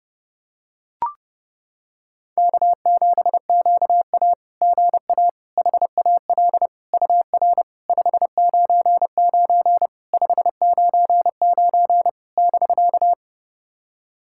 Morse code, heard as a keyed tone, about 30 wpm, 690 Hz. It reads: K7QA GA HAL UR 599 599 <BK>
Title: SOTA QSO Practice 30wpm